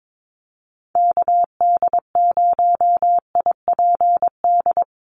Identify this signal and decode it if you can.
Morse code, heard as a keyed tone, about 22 wpm, 700 Hz. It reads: KD0IPD
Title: USA Call Signs 22wpm